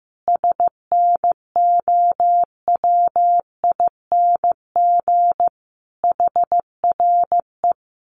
Morse code, heard as a keyed tone, about 15 wpm, 700 Hz. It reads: SNOWING HRE
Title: Common QSO Phrases 15wpm